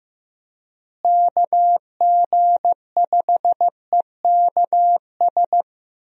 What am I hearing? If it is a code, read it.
Morse code, heard as a keyed tone, about 15 wpm, 700 Hz. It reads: KG5EKS
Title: USA Call Signs - 6 Characters - Repeated 2x - 15wpm